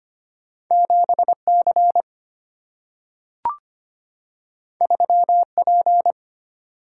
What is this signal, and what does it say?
Morse code, heard as a keyed tone, about 25 wpm, 700 Hz. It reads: 7C 3P
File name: Two Letter-Number Combinations - 25wpm